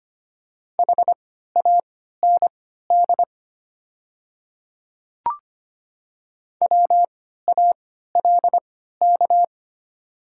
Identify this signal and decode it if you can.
Morse code, heard as a keyed tone, about 25 wpm, spaced as for 14 wpm, 700 Hz. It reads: HAND WALK
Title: 3x Character Spacing - Top 300 Words - 25wpm